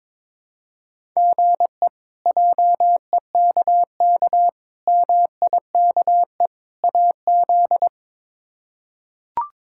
Morse code, heard as a keyed tone, about 22 wpm, 700 Hz. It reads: GE JEKK MIKE AZ
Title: K1USN SST QSO Practice - 22wpm